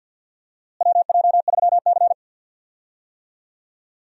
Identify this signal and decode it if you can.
Morse code, heard as a keyed tone, about 52 wpm, 700 Hz. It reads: WJ3C
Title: USA Call Signs - 4 Characters - Repeated 2x - 50wpm